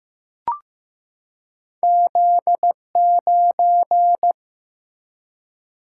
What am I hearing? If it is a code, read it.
Morse code, heard as a keyed tone, about 15 wpm, 700 Hz. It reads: Z9
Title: Single-Letter Single-Number 15wpm